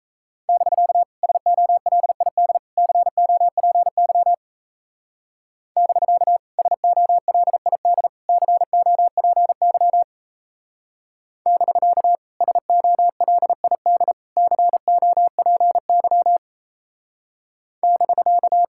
Morse code, heard as a keyed tone, about 35 wpm, 700 Hz. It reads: <BK> SOLID COPY <BK> SOLID COPY <BK> SOLID COPY <BK>
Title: Speed-Racing Common QSO Phrases Encore - 28wpm